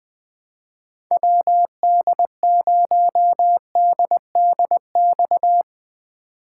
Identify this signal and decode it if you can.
Morse code, heard as a keyed tone, about 20 wpm, 700 Hz. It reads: WD0DDX
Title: USA Call Signs - 6 Characters - 2x Character Spacing - 20wpm